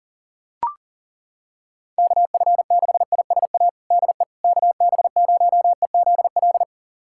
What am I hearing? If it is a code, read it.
Morse code, heard as a keyed tone, about 40 wpm, 700 Hz. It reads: KF6ISA DE KB0EZL
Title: USA Callsign DE Callsign - 6 Characters - 40wpm